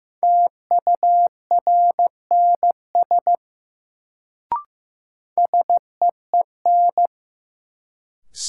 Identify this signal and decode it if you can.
Morse code, heard as a keyed tone, about 15 wpm, 700 Hz. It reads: TURNS SEEN
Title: Sets of 5 Words Encore - New Word Drill - 15wpm